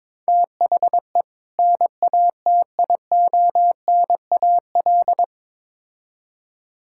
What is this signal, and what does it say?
Morse code, heard as a keyed tone, about 22 wpm, 700 Hz. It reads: THE NATIONAL
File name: Build A Phrase - 2-Word Phrases - 22wpm